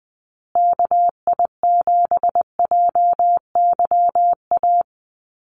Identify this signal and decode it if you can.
Morse code, heard as a keyed tone, about 20 wpm, 700 Hz. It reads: KI7JYA
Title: USA Call Signs Repeated 3x 20wpm